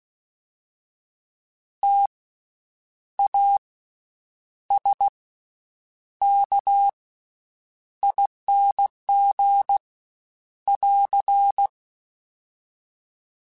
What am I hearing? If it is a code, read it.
Morse code, heard as a keyed tone, about 16 wpm, spaced as for 8 wpm, 800 Hz. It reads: TASKING <AR>
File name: Special INGs / Special INGs-mp3s / VerySlowToFaster Ing @15